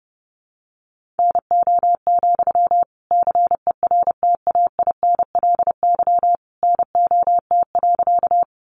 Morse code, heard as a keyed tone, about 30 wpm, 700 Hz. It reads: NO, CERTAINLY NOT.
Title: Sentences from Top 900 Words 30wpm